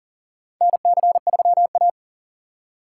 Morse code, heard as a keyed tone, about 40 wpm, 700 Hz. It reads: NC3A
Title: USA Call Signs - 6x Character Spacing - 40wpm